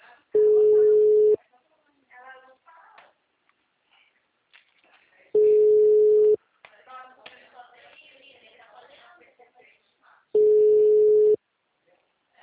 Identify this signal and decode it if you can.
Ringback tone (Europe, 425 Hz), with speech.